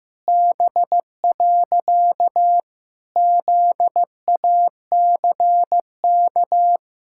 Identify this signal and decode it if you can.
Morse code, heard as a keyed tone, about 15 wpm, 700 Hz. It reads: B. ZACK
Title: Pangram Challenge v2 - 15wpm